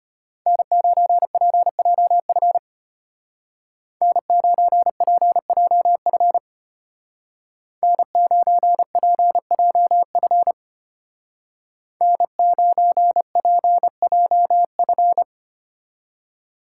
Morse code, heard as a keyed tone, about 30 wpm, 700 Hz. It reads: N9PJF N9PJF N9PJF N9PJF
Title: Speed Racing - USA 1x3 Call Signs 25wpm